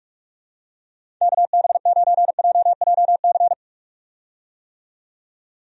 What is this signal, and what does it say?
Morse code, heard as a keyed tone, about 46 wpm, 700 Hz. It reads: KB9JJC
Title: USA Call Signs - 6 Characters - 45wpm